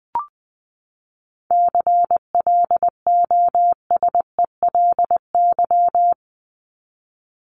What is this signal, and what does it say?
Morse code, heard as a keyed tone, about 20 wpm, 700 Hz. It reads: CLOSELY
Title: Top 1251-1500 Words 20wpm